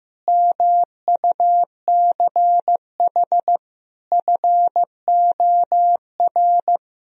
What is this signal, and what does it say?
Morse code, heard as a keyed tone, about 15 wpm, 700 Hz. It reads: MUCH FOR
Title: Speed-Racing - Sets of 4 Words 15wpm